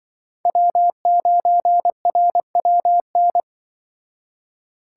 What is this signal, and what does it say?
Morse code, heard as a keyed tone, about 24 wpm, 700 Hz. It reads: W9RWN
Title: Speed Racing - USA 1x3 Call Signs 20wpm